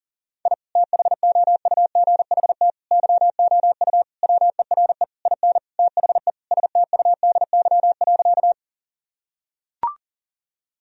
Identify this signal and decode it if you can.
Morse code, heard as a keyed tone, about 37 wpm, 700 Hz. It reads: I THOUGHT YOU WERE IN THE STUDY.
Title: Sentences from Top 1000 Words - Review Part 1 - 40wpm